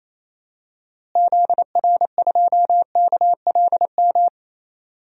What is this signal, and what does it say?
Morse code, heard as a keyed tone, about 28 wpm, 700 Hz. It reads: ZR2KLM
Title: DXCC-100 Call Signs - 6 Characters - Repeated 2x - 28wpm